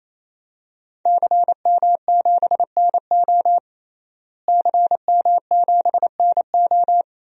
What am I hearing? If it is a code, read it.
Morse code, heard as a keyed tone, about 28 wpm, 700 Hz. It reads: CM7NO CM7NO
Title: DXCC-100 Call Signs - 5 Characters - Repeated 3x - 28wpm